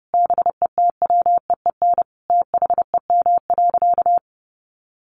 Morse code, heard as a keyed tone, about 30 wpm, 700 Hz. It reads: BETWEEN THEM.
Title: Sentences from Top 400 Words 30wpm